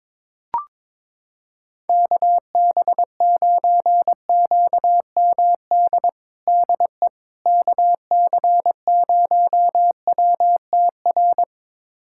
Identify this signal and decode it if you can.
Morse code, heard as a keyed tone, about 22 wpm, 700 Hz. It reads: KB9QMD DE KC0WTR
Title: USA Callsign DE Callsign 22wpm